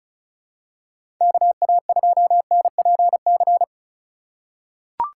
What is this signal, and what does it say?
Morse code, heard as a keyed tone, about 36 wpm, 700 Hz. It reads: KA2NPC
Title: USA Call Signs - 6 Characters - 2x Character Spacing - 35wpm